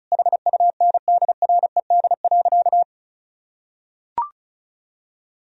Morse code, heard as a keyed tone, about 34 wpm, 700 Hz. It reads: HUNDRED.
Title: Sentences from Top 1000 Words - Review Part 2 - 35wpm